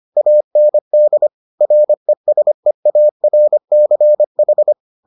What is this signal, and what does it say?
Morse code, heard as a keyed tone, about 25 wpm, 600 Hz. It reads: AND RESEARCH